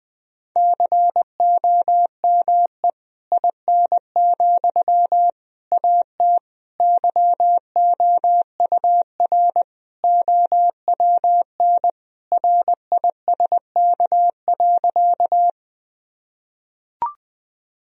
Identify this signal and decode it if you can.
Morse code, heard as a keyed tone, about 21 wpm, 700 Hz. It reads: COME IN, AT YOUR OWN RISK.
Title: Sentences from Top 1000 Words - Review Part 4 - 20wpm